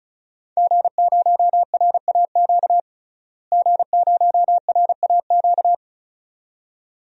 Morse code, heard as a keyed tone, about 36 wpm, 700 Hz. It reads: G0RAQ G0RAQ
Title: DXCC-100 Call Signs - 5 Characters - Repeated 2x - 35wpm